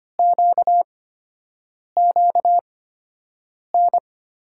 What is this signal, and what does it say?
Morse code, heard as a keyed tone, about 25 wpm, spaced as for 7 wpm, 700 Hz. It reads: QQN
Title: USA Call Signs - 8x Character Spacing - 25wpm